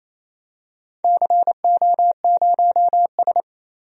Morse code, heard as a keyed tone, about 28 wpm, 700 Hz. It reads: CO0S
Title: DXCC-100 Call Signs - 4 Characters - 28wpm